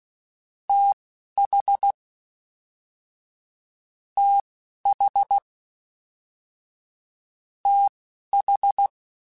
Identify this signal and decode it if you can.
Morse code, heard as a keyed tone, about 16 wpm, spaced as for 11 wpm, 800 Hz. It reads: TH TH TH